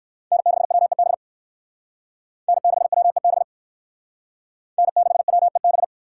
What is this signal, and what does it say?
Morse code, heard as a keyed tone, about 59 wpm, 700 Hz. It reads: N6WEB N6WEB N6WEB